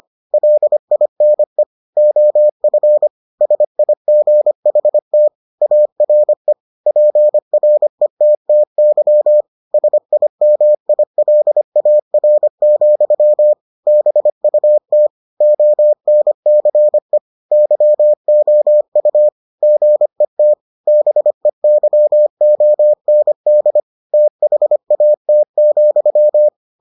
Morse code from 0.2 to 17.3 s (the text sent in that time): LINE OF SIGHT ARE PRETTY SIMILAR, BUT ONCE